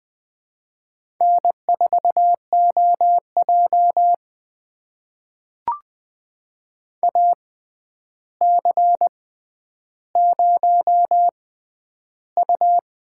Morse code, heard as a keyed tone, about 20 wpm, spaced as for 10 wpm, 700 Hz. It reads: N4OJ AC0U
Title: USA Call Signs - 4 Characters - 6x Character Spacing - 20wpm